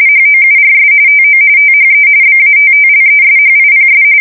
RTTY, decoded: HOURS TO THE BATTLESHIPS

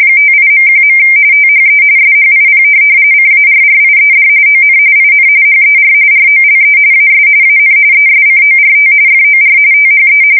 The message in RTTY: 58.1. THE SHIP CARRIED ON IN HER ROLE AS AN ESCORT FOR THE

